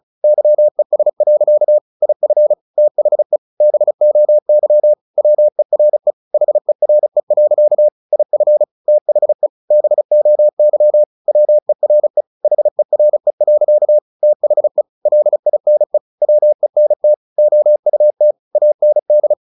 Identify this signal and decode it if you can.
Morse code, heard as a keyed tone, about 34 wpm, 600 Hz. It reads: YES. IF THE BOY WERE HERE. IF THE BOY WERE HERE. THE LINE WENT OUT AND
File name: the old man and the sea 2 35wpm